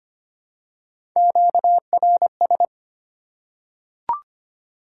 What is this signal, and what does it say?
Morse code, heard as a keyed tone, about 25 wpm, 700 Hz. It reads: QRS
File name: Top 90 QSO Elements - 6x Character Spacing - 25wpm